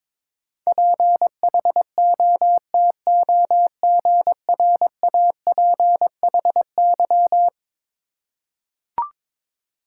Morse code, heard as a keyed tone, about 22 wpm, 700 Hz. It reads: PHOTOGRAPHY
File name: practice / 22wpm / Eleven Letter Words - 22wpm